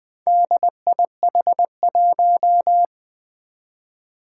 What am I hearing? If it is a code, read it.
Morse code, heard as a keyed tone, about 20 wpm, 700 Hz. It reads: DIH1
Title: Four Challenging Letter-Number Combinations 20wpm